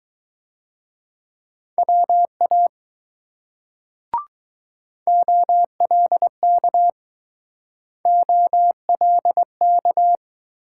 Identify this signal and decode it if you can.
Morse code, heard as a keyed tone, about 20 wpm, 700 Hz. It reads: WA OLK OLK